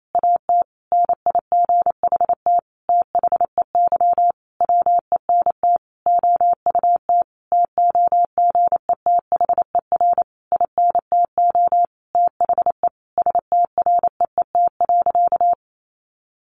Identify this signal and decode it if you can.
Morse code, heard as a keyed tone, about 28 wpm, 700 Hz. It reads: AT NIGHT THEY WENT OUT TOGETHER INTO THE STREET.
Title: Sentences from Top 600 Words 28wpm